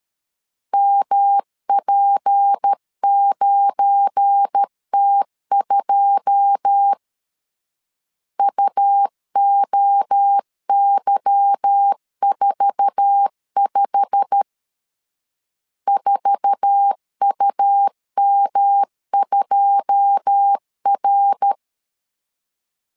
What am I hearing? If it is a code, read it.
Morse code, heard as a keyed tone, about 13 wpm, 800 Hz. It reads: MP9T2 UOY45 4UM2R
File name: LEZIONI / 9 2 12wpm 2301